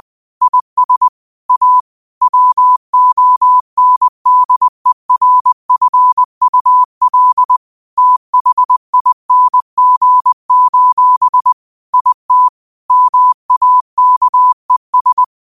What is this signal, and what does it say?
Morse code, heard as a keyed tone, about 20 wpm, 1000 Hz. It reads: IS A WONDERFUL THING: IT MAKES
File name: qotdfiles / 20wpm